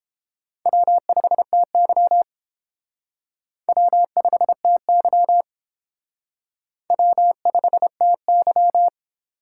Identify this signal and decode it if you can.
Morse code, heard as a keyed tone, about 31 wpm, 700 Hz. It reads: W5TY W5TY W5TY